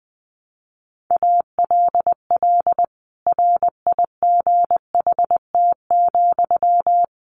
Morse code, heard as a keyed tone, about 20 wpm, 700 Hz. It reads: ALL RIGHT,
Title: A Sound Of Thunder 20wpm